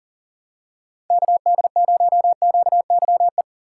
Morse code, heard as a keyed tone, about 40 wpm, 700 Hz. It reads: KD0QYE
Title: USA Call Signs - 6 Characters - 3x Character Spacing - 40wpm